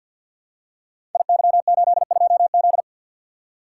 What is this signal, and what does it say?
Morse code, heard as a keyed tone, about 50 wpm, 700 Hz. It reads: IY82Z
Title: Code Groups - 5 Letter:Numbers - 50wpm